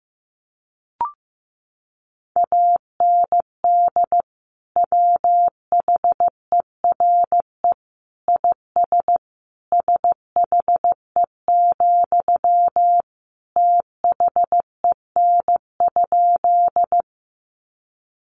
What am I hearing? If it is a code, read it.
Morse code, heard as a keyed tone, about 15 wpm, 700 Hz. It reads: AND WHERE IS SHE, THEN?
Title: Sentences from Top 200 Words 15wpm